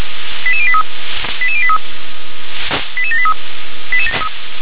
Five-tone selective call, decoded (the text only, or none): 88083 88083 88783